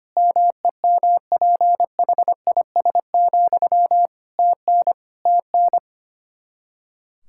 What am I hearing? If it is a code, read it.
Morse code, heard as a keyed tone, about 25 wpm, 700 Hz. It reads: MEMPHIS, TN TN
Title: Practice QSO - Exchange 1 - Beginning - 25wpm